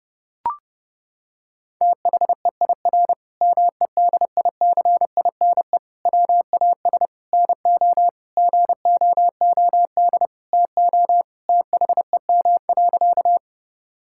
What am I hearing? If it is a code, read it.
Morse code, heard as a keyed tone, about 32 wpm, 700 Hz. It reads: THEIR MEDICINE WAS NO GOOD TO THEM.
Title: Sentences from Top 1000 Words - Review Part 6 - 30wpm